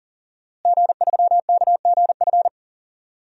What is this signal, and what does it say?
Morse code, heard as a keyed tone, about 39 wpm, 700 Hz. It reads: G3KGF
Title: DXCC-100 Call Signs - 5 Characters - Repeated 2x - 40wpm